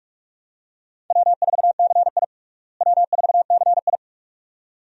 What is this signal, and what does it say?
Morse code, heard as a keyed tone, about 47 wpm, 700 Hz. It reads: W4CI W4CI
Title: USA Call Signs - 4 Characters - Repeated 2x - 45wpm